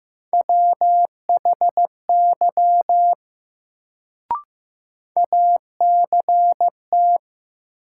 Morse code, heard as a keyed tone, about 15 wpm, 700 Hz. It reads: WHY ACT
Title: Sentences from Top 200 Words - New Word Drill - 15wpm